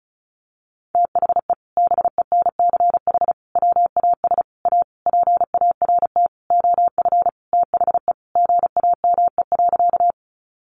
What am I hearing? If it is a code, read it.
Morse code, heard as a keyed tone, about 34 wpm, 700 Hz. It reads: THE BENCH WAS A PART OF THE GAME.
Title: Sentences from Top 1400 Words 35wpm